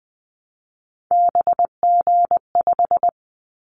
Morse code, heard as a keyed tone, about 20 wpm, 700 Hz. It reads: BG5
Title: Callsign Prefixes for Top 100 DXCC - 20wpm